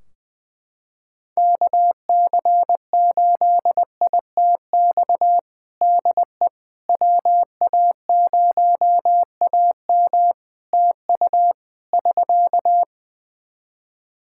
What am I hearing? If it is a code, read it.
Morse code, heard as a keyed tone, about 20 wpm, 700 Hz. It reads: KC8ITX DE WA0AM TU <SK>
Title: Practice QSO - Exchange 4 - Ending - 20wpm